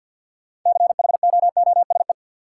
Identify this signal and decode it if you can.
Morse code, heard as a keyed tone, about 52 wpm, 700 Hz. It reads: CHOOSE
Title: Sentences from Top 700 Words - New Word Drill - 50wpm